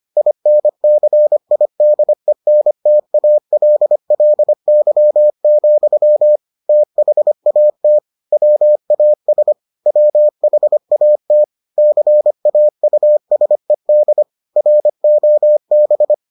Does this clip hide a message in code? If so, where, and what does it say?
Yes: Morse code from 0.2 to 16.1 s, heard as a keyed tone, about 25 wpm, 600 Hz. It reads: INCIDENTALLY, THAT WAS WHAT CAUSED ROB